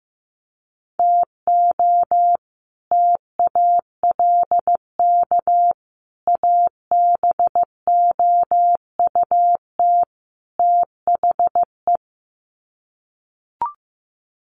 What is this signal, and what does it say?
Morse code, heard as a keyed tone, about 15 wpm, 700 Hz. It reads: TO TALK ABOUT THE
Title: Sets of 4 Words 15wpm